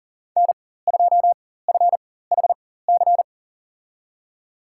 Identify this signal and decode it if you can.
Morse code, heard as a keyed tone, about 37 wpm, spaced as for 18 wpm, 700 Hz. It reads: N2FHC
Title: USA Call Signs - 5 Characters - 4x Character Spacing - 40wpm